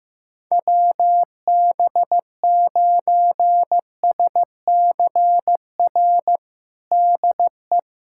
Morse code, heard as a keyed tone, about 15 wpm, 700 Hz. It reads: WB9SCR DE
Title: USA Callsign DE Callsign 15wpm